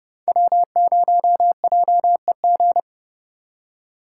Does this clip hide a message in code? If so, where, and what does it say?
Yes: Morse code from 0.3 to 2.8 s, heard as a keyed tone, about 30 wpm, 700 Hz. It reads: W0JEG